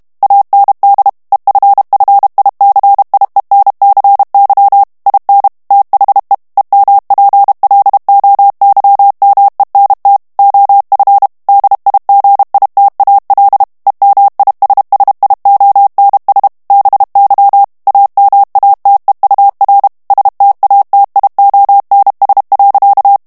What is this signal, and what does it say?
Morse code, heard as a keyed tone, about 32 wpm, 800 Hz. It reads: AND EFFICIENCY IN THE EMPLOYMENT OF DIGITAL EMISSIONS BY AMATEUR STATIONS.
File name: morse mp3 / ANGLAIS texte clair 3 vitesse 30 mots minute